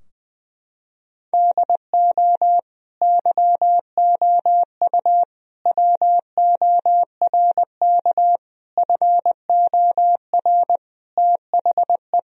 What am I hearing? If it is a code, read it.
Morse code, heard as a keyed tone, about 21 wpm, 700 Hz. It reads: DO YOU WORK FOR THE
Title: Sentences from Top 1000 Words - Review Part 4 - 20wpm